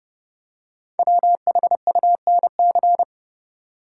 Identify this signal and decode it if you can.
Morse code, heard as a keyed tone, about 29 wpm, 700 Hz. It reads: WHUNC
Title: Code Groups - 5 Letters - 30wpm